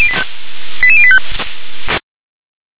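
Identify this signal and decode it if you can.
Five-tone selective call (ZVEI), heard as quick runs of five tones, tones of about 70 ms. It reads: 88085
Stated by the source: ZVEI five-tone call received by SDR.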